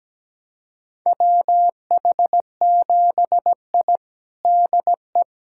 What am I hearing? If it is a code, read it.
Morse code, heard as a keyed tone, about 17 wpm, 700 Hz. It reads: WH7I DE
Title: USA Callsign DE Callsign - 4 Characters - 17wpm